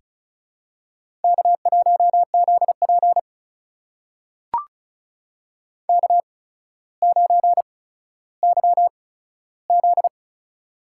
Morse code, heard as a keyed tone, about 36 wpm, spaced as for 15 wpm, 700 Hz. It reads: K1ZP K9YZ
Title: USA Call Signs - 4 Characters - 8x Character Spacing - 35wpm